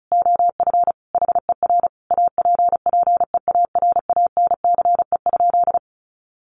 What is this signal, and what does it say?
Morse code, heard as a keyed tone, about 34 wpm, 700 Hz. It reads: OF HER APPEARANCE?
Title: Sentences from Top 1100 Words 35wpm